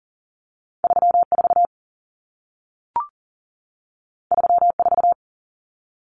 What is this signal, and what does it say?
Morse code, heard as a keyed tone, about 40 wpm, 700 Hz. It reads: 34 34
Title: Common Cut Numbers - 2-Digit 40wpm